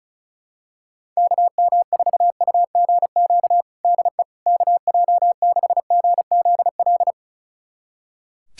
Morse code, heard as a keyed tone, about 36 wpm, 700 Hz. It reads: KM4UGQ DE KJ6GZL
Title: USA Callsign DE Callsign 35wpm